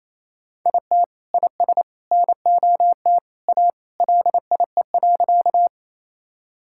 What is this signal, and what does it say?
Morse code, heard as a keyed tone, about 27 wpm, 700 Hz. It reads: IT IS NOT A LIE.
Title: Sentences from Top 1600 Words 28wpm